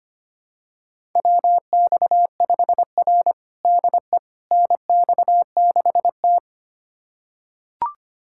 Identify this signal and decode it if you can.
Morse code, heard as a keyed tone, about 25 wpm, 700 Hz. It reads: WX5R DE NX6T
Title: USA Callsign DE Callsign - 4 Characters - 25wpm